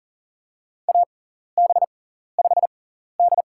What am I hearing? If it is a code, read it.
Morse code, heard as a keyed tone, about 40 wpm, spaced as for 14 wpm, 700 Hz. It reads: AB5D
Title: USA Call Signs - 4 Characters - 6x Character Spacing - 40wpm